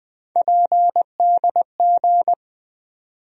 Morse code, heard as a keyed tone, about 20 wpm, 700 Hz. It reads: PDG
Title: USA Call Sign Postfixes - 3 Letters 20wpm